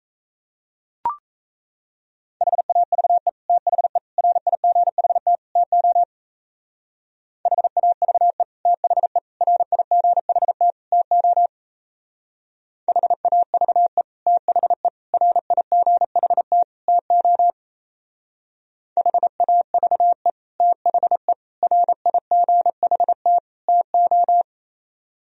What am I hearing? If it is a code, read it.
Morse code, heard as a keyed tone, about 33 wpm, 700 Hz. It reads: HAVE THE RIGHT TO HAVE THE RIGHT TO HAVE THE RIGHT TO HAVE THE RIGHT TO
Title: Speed-Racing - Sets of 4 Words 28wpm